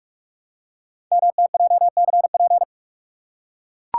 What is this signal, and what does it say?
Morse code, heard as a keyed tone, about 44 wpm, 690 Hz. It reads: MTJCP
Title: Code Groups - 5 Letters - 45wpm